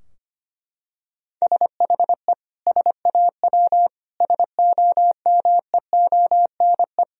Morse code, heard as a keyed tone, about 25 wpm, 700 Hz. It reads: SHE SAW SOMEONE
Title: Sentences from Top 1000 Words - Review Part 4 - 25wpm